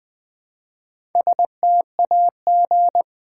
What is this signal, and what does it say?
Morse code, heard as a keyed tone, about 20 wpm, 700 Hz. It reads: STAG
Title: Build A Word - 1 Letter At A Time - 3 to 5-Letter Words - 20wpm